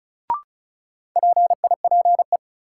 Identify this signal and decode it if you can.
Morse code, heard as a keyed tone, about 36 wpm, 700 Hz. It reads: PIPE
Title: Sentences from Top 1500 Words - New Word Drill Alt - 35wpm